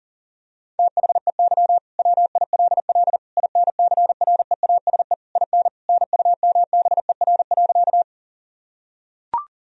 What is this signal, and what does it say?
Morse code, heard as a keyed tone, about 40 wpm, 700 Hz. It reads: THEY WILL INCREASE IN NUMBER.
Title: Sentences from Top 1000 Words - Review Part 4 - 40wpm